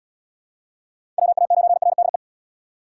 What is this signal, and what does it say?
Morse code, heard as a keyed tone, about 68 wpm, 700 Hz. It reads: WA9RDE